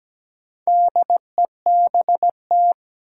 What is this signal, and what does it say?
Morse code, heard as a keyed tone, about 17 wpm, 700 Hz. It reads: DEBT
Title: Sentences from Top 1300 Words - New Word Drill Alt - 17wpm